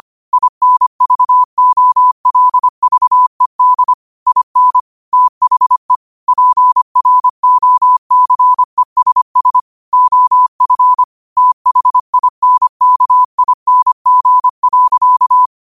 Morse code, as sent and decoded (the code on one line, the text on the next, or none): .. -. ...- --- .-.. ...- . -.. / .. -. / - .... / .--. .-. --- -.-. ... ... / --- ..-. / - .... .. -. -.- .. -. --. .-.-.-
INVOLVED IN THE PROCESS OF THINKING.